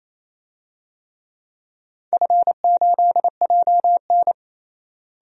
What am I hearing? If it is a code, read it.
Morse code, heard as a keyed tone, about 28 wpm, 700 Hz. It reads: F8JN
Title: DXCC-100 Call Signs - 4 Characters - Repeated 3x - 28wpm